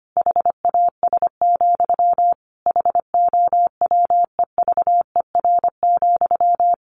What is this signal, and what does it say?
Morse code, heard as a keyed tone, about 25 wpm, 700 Hz. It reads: HAS, HOWEVER,
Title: Sentences from Top 1200 Words 25wpm